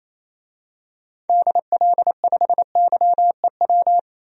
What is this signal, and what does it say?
Morse code, heard as a keyed tone, about 28 wpm, 700 Hz. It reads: DL5YEW
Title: DXCC-100 Call Signs - 6 Characters - 28wpm